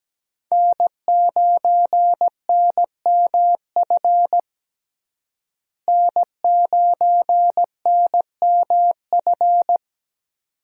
Morse code, heard as a keyed tone, about 17 wpm, 700 Hz. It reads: N9NMF N9NMF
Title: USA Call Signs - 5 Characters - Repeated 2x - 17wpm